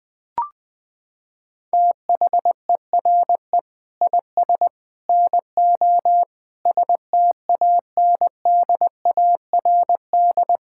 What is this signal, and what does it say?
Morse code, heard as a keyed tone, about 20 wpm, 700 Hz. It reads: THERE IS NO STANDARD
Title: Sentences from Top 1600 Words 20wpm